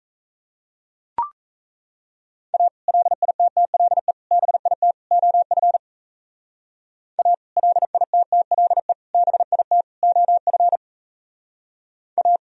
Morse code, heard as a keyed tone, about 34 wpm, 700 Hz. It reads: A LITTLE BIT OF A LITTLE BIT OF A
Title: Speed-Racing - Sets of 4 Words 28wpm